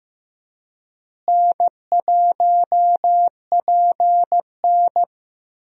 Morse code, heard as a keyed tone, about 15 wpm, 700 Hz. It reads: N1PN
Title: USA Call Signs - 4 Characters - Repeated 2x - 15wpm